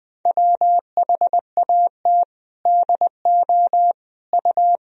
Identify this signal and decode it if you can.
Morse code, heard as a keyed tone, about 21 wpm, 700 Hz. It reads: WHAT DO U
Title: Sentences from Top 100 Words and Abbreviations 20wpm